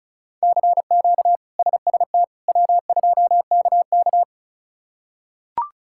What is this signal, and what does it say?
Morse code, heard as a keyed tone, about 36 wpm, 700 Hz. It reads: CQ SST W2KK
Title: K1USN SST QSO Practice - 35wpm